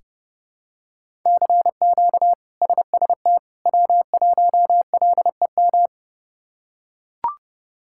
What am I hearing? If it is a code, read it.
Morse code, heard as a keyed tone, about 30 wpm, 700 Hz. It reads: CQ SST W1LEM
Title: K1USN SST QSO Practice - 30wpm